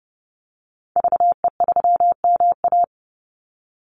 Morse code, heard as a keyed tone, about 30 wpm, 700 Hz. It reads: VE3MA
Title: Call Signs Repeated 3x - 30wpm